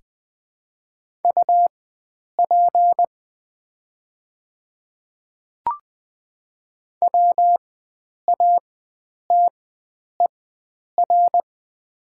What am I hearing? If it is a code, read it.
Morse code, heard as a keyed tone, about 20 wpm, spaced as for 9 wpm, 700 Hz. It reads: UP WATER
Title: 4x Character Spacing - Top 300 Words - 20wpm